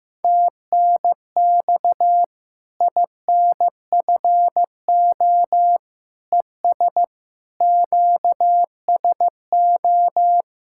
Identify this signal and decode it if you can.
Morse code, heard as a keyed tone, about 16 wpm, 700 Hz. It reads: TNX INFO ES QSO
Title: Practice QSO - Exchange 4 - Ending - 15wpm